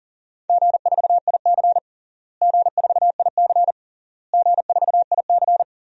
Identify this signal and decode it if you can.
Morse code, heard as a keyed tone, about 40 wpm, 700 Hz. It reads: G4IC G4IC G4IC
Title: DXCC-100 Call Signs - 4 Characters - Repeated 3x - 40wpm